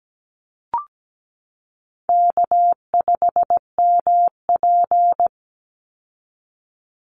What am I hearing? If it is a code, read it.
Morse code, heard as a keyed tone, about 17 wpm, 700 Hz. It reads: K5MP
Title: Easy Call Signs Repeated 2x - 17wpm